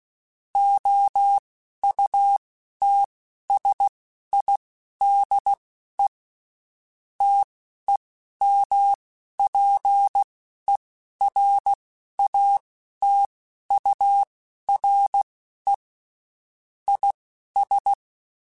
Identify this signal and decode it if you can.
Morse code, heard as a keyed tone, about 16 wpm, spaced as for 11 wpm, 800 Hz. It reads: OUTSIDE TEMPERATURE IS
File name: Spaced Characters / Spaced Characters-mp3s / Weather4 24 sp @15